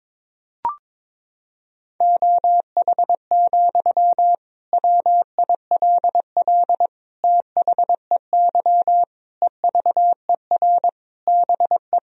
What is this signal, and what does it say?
Morse code, heard as a keyed tone, about 21 wpm, 700 Hz. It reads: OH, WILL THEY EVER BE
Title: Sentences from Top 1000 Words - Review Part 4 - 22wpm